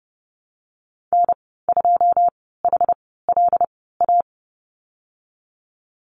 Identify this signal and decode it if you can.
Morse code, heard as a keyed tone, about 30 wpm, spaced as for 17 wpm, 700 Hz. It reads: N2HLA
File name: Lesson 19g - Letter w - Call Signs 30wpm